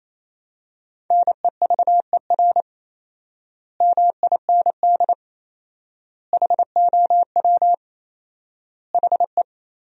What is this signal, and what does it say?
Morse code, heard as a keyed tone, about 28 wpm, 700 Hz. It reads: NEVER MIND HOW HE
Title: 4x Word-Spacing - Sentences from Top 500 Words - 28wpm